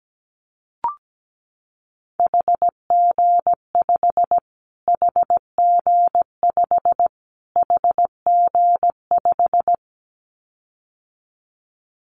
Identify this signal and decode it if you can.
Morse code, heard as a keyed tone, about 17 wpm, 700 Hz. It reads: HG5 HG5 HG5
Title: Callsign Prefixes for Top 25 DXCC - Repeated 3x - 17wpm